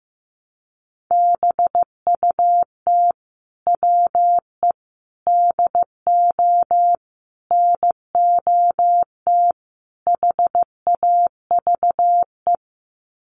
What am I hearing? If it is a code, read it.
Morse code, heard as a keyed tone, about 15 wpm, 700 Hz. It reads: BUT WE DO NOT HAVE
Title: Sets of 5 Words Encore2 15wpm